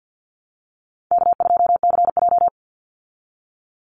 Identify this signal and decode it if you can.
Morse code, heard as a keyed tone, about 50 wpm, 700 Hz. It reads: K2CJ
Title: Call Signs Repeated 2x - 50wpm